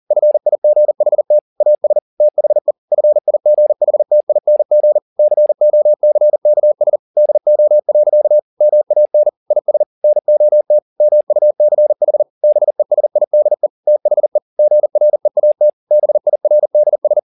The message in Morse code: FIGHT AS THE FIGHTING COCKS DO. MAN IS NOT MUCH BESIDE THE GREAT BIRDS

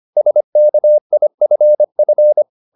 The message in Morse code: SKIFF

